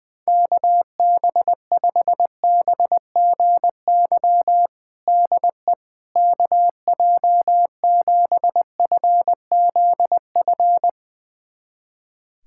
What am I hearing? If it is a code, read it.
Morse code, heard as a keyed tone, about 20 wpm, 700 Hz. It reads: KB5BGY DE KJ7FZF
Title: USA Callsign DE Callsign 20wpm